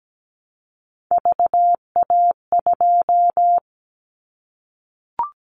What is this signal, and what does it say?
Morse code, heard as a keyed tone, about 17 wpm, 700 Hz. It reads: VA2
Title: Callsign Prefixes for Top 25 DXCC - Repeated 3x - 17wpm